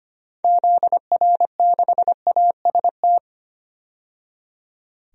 Morse code, heard as a keyed tone, about 25 wpm, 700 Hz. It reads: ZR6AST
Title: DXCC-100 Call Signs - 6 Characters - 25wpm